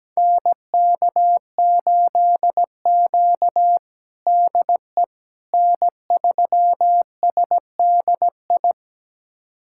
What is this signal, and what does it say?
Morse code, heard as a keyed tone, about 17 wpm, 700 Hz. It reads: NK8Q DE N3SDI
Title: USA Callsign DE Callsign 17wpm